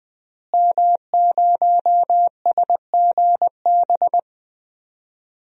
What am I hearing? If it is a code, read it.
Morse code, heard as a keyed tone, about 20 wpm, 700 Hz. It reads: M0SGB
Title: DXCC-100 Call Signs - 5 Characters - Repeated 3x - 20wpm